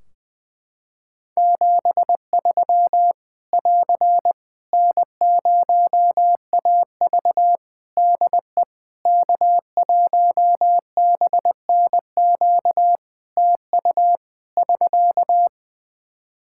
Morse code, heard as a keyed tone, about 20 wpm, 700 Hz. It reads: 73 <AR> N0AV DE K1BNQ TU <SK>
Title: Practice QSO - Exchange 4 - Ending - 20wpm